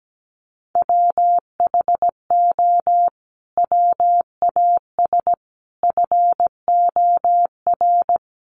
Morse code, heard as a keyed tone, about 17 wpm, 700 Hz. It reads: WHO WAS FOR
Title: Sentences from Top 1400 Words 17wpm